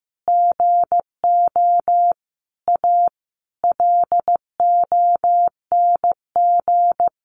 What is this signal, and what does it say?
Morse code, heard as a keyed tone, about 15 wpm, 700 Hz. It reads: GO A LONG
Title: Sentences from Top 1200 Words 15wpm